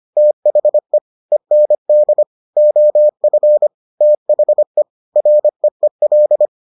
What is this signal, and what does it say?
Morse code, heard as a keyed tone, about 25 wpm, 600 Hz. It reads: THE END OF THE REEL